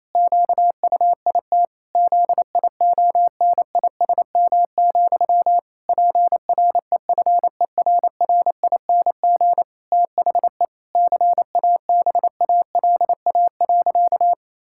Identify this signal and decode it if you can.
Morse code, heard as a keyed tone, about 28 wpm, 700 Hz. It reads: QUIT ZIONISM, PREFERRING THE CABALA.
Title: Pangram Challenge v2 - 28wpm